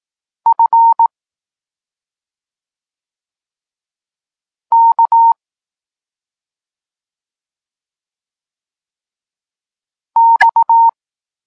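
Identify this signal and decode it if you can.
Morse code, heard as a keyed tone, about 18 wpm, 920 Hz. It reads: F K X
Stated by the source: Morse code lesson for learning the letters K and X, from the SP9MOA Niepołomicki Klub Krótkofalowców course.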